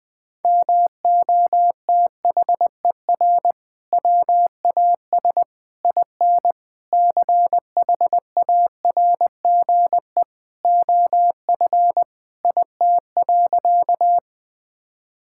Morse code, heard as a keyed tone, about 20 wpm, 700 Hz. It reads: MOTHER WAS IN CHARGE OF IT.
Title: Sentences from Top 1000 Words - Review Part 6 - 20wpm